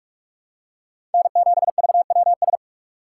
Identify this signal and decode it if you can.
Morse code, heard as a keyed tone, about 44 wpm, 700 Hz. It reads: N7VWS
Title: USA Call Signs - 5 Characters - 3x Character Spacing - 45wpm